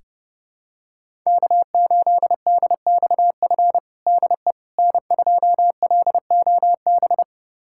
Morse code, heard as a keyed tone, about 31 wpm, 700 Hz. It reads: K8DXF DE N2LOB
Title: USA Callsign DE Callsign - 5 Characters - 30wpm